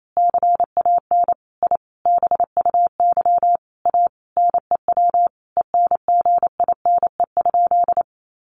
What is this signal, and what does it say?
Morse code, heard as a keyed tone, about 28 wpm, 700 Hz. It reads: CAN I BUY A NEW ENGINE?
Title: Sentences from Top 1500 Words 28wpm